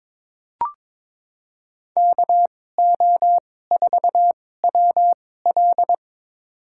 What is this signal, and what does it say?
Morse code, heard as a keyed tone, about 22 wpm, spaced as for 16 wpm, 700 Hz. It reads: KO4WL
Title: USA Call Signs - 5 Characters - 2x Character Spacing - 22wpm